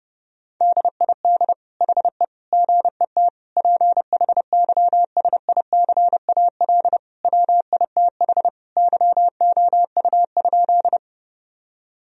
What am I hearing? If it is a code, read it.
Morse code, heard as a keyed tone, about 31 wpm, 700 Hz. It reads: DID HE GET PHYSICAL WITH YOU?
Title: Sentences from Top 1000 Words - Review Part 4 - 30wpm